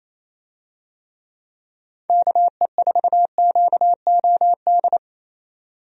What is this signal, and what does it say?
Morse code, heard as a keyed tone, about 28 wpm, 700 Hz. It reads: KE4QOD
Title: USA Call Signs - 6 Characters - 28wpm